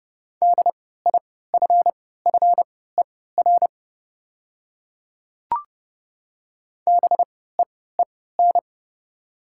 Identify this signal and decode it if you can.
Morse code, heard as a keyed tone, about 30 wpm, spaced as for 17 wpm, 690 Hz. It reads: DIFFER BEEN
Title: 3x Character Spacing - Top 200 Words - 30wpm